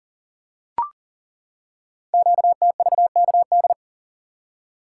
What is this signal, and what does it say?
Morse code, heard as a keyed tone, about 40 wpm, 700 Hz. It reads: QTVKD
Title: Code Groups - 5 Letter:Numbers - 40wpm